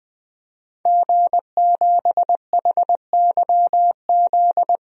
Morse code, heard as a keyed tone, about 20 wpm, 700 Hz. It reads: G7HYZ
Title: DXCC-100 Call Signs - 5 Characters - 20wpm